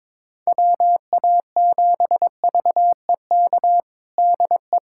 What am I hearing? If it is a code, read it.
Morse code, heard as a keyed tone, about 22 wpm, 700 Hz. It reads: WA7VEK DE
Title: USA Callsign DE Callsign - 6 Characters - 22wpm